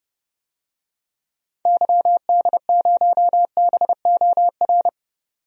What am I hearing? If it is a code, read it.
Morse code, heard as a keyed tone, about 30 wpm, 700 Hz. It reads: YD0BOR
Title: DXCC-100 Call Signs - 6 Characters - 30wpm